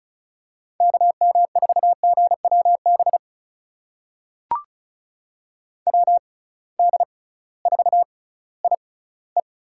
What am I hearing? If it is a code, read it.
Morse code, heard as a keyed tone, about 36 wpm, spaced as for 19 wpm, 700 Hz. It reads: KM4GWB WD4IE
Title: USA Call Signs - 6x Character Spacing - 35wpm